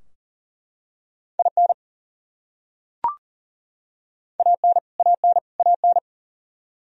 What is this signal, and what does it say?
Morse code, heard as a keyed tone, about 40 wpm, 700 Hz. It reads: IN AN AN AN
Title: Top 100 Words Repeated 40wpm